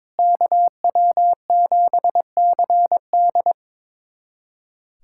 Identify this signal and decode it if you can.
Morse code, heard as a keyed tone, about 21 wpm, 700 Hz. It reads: KW7CD
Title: USA Callsign DE Callsign - 5 Characters - 22wpm